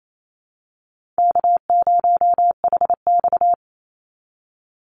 Morse code, heard as a keyed tone, about 28 wpm, 700 Hz. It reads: K0HX
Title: Easy Call Signs Repeated 2x - 28wpm